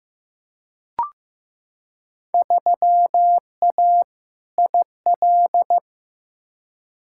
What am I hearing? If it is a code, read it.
Morse code, heard as a keyed tone, about 16 wpm, 700 Hz. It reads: 3A IL
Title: Field Day Exchanges - 15wpm